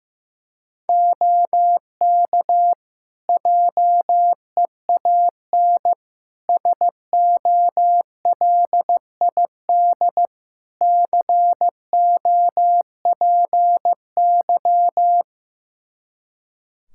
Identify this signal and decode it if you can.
Morse code, heard as a keyed tone, about 15 wpm, 700 Hz. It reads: OK JEAN SOLID COPY
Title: Practice QSO - Exchange 3 - More Info - 15wpm